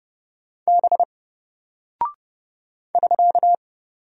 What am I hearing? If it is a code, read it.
Morse code, heard as a keyed tone, about 30 wpm, 700 Hz. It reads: B <SK>
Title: Single Character (Letter:Number:Punctuation:Prosign) 30wpm